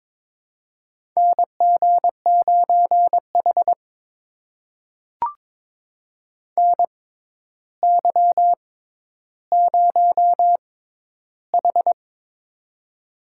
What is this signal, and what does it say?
Morse code, heard as a keyed tone, about 23 wpm, spaced as for 11 wpm, 700 Hz. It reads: NG9H NY0H
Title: USA Call Signs - 4 Characters - 6x Character Spacing - 22wpm